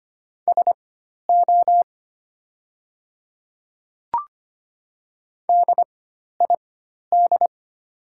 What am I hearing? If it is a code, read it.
Morse code, heard as a keyed tone, about 25 wpm, spaced as for 12 wpm, 700 Hz. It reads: SO DID
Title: 4x Character Spacing - Top 300 Words - 25wpm